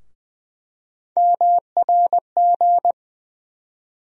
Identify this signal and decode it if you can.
Morse code, heard as a keyed tone, about 20 wpm, 700 Hz. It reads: MRG
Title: USA Call Sign Postfixes - 3 Letters 20wpm